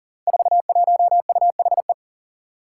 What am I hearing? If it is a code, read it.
Morse code, heard as a keyed tone, about 38 wpm, 700 Hz. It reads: 41UHE
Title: Five Challenging Letter-Number Combinations 40wpm